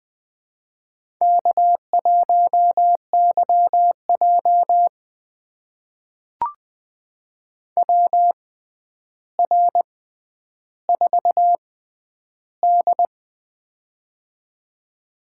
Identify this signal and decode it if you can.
Morse code, heard as a keyed tone, about 20 wpm, spaced as for 10 wpm, 700 Hz. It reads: K1YJ WR4D
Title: USA Call Signs - 4 Characters - 6x Character Spacing - 20wpm